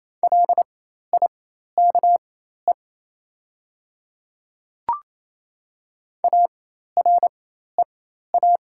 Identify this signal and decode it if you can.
Morse code, heard as a keyed tone, about 28 wpm, spaced as for 13 wpm, 700 Hz. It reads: LIKE AREA
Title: 4x Character Spacing - Top 500 Words - 28wpm